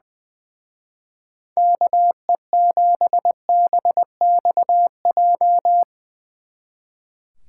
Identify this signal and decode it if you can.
Morse code, heard as a keyed tone, about 20 wpm, 700 Hz. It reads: KE7BXJ
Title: Speed Racing - USA Call Signs 20wpm